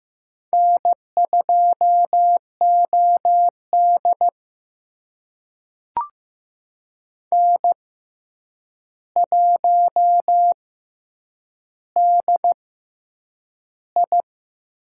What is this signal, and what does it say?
Morse code, heard as a keyed tone, about 15 wpm, spaced as for 8 wpm, 700 Hz. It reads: N2OD N1DI
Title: USA Call Signs - 4 Characters - 6x Character Spacing - 15wpm